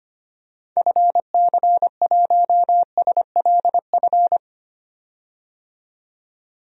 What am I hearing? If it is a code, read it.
Morse code, heard as a keyed tone, about 25 wpm, 700 Hz. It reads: FC1SLF
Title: DXCC-100 Call Signs - 6 Characters - 25wpm